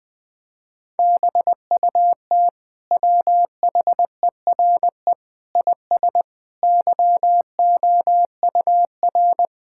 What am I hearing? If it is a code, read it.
Morse code, heard as a keyed tone, about 20 wpm, 700 Hz. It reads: BUT WHERE IS YOUR
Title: Sentences from Top 1000 Words - Review Part 5 - 20wpm